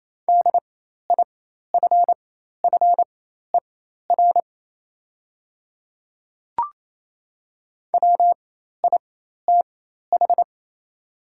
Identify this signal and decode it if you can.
Morse code, heard as a keyed tone, about 28 wpm, spaced as for 13 wpm, 700 Hz. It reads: DIFFER WITH
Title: 4x Character Spacing - Top 200 Words - 28wpm